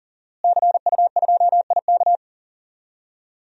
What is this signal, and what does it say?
Morse code, heard as a keyed tone, about 40 wpm, 700 Hz. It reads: CU2IK
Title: DXCC-100 Call Signs - 5 Characters - Repeated 2x - 40wpm